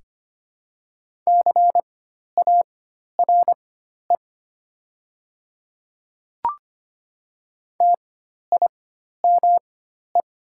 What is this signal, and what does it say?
Morse code, heard as a keyed tone, about 25 wpm, spaced as for 12 wpm, 700 Hz. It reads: CARE TIME
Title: 4x Character Spacing - Top 500 Words - 25wpm